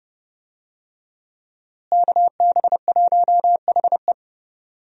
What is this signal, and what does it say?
Morse code, heard as a keyed tone, about 30 wpm, 700 Hz. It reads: KB1HE
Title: USA Call Signs - 5 Characters - Repeated 3x - 30wpm